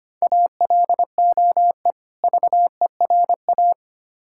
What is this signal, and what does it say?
Morse code, heard as a keyed tone, about 25 wpm, 700 Hz. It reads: ALOE VERA
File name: practice / 25wpm / Difficult Words with Hints - 25wpm